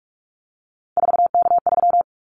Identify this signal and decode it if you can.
Morse code, heard as a keyed tone, about 44 wpm, 700 Hz. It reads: 4K3
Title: Callsign Prefixes for Top 100 DXCC - 45wpm